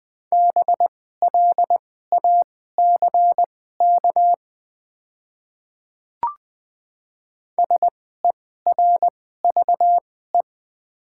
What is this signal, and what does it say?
Morse code, heard as a keyed tone, about 20 wpm, spaced as for 14 wpm, 700 Hz. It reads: BLACK SERVE
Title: 2x Character Spacing - Top 500 Words - 20wpm